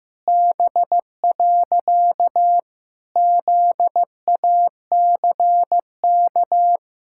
Morse code, heard as a keyed tone, about 16 wpm, 700 Hz. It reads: B. ZACK
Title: Pangram Challenge v2 - 15wpm